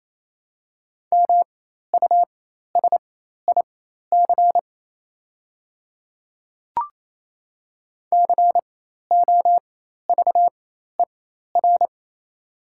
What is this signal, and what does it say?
Morse code, heard as a keyed tone, about 28 wpm, spaced as for 13 wpm, 700 Hz. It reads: MUSIC COVER
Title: 4x Character Spacing - Top 300 Words - 28wpm